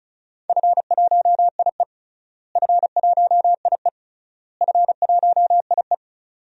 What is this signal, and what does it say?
Morse code, heard as a keyed tone, about 34 wpm, 700 Hz. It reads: F1IE F1IE F1IE
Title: DXCC-100 Call Signs - 4 Characters - Repeated 3x - 35wpm